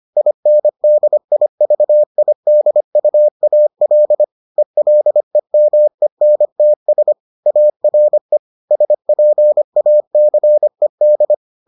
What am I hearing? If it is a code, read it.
Morse code, heard as a keyed tone, about 25 wpm, 600 Hz. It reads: INDIVIDUAL ELEMENTS ARE SPACED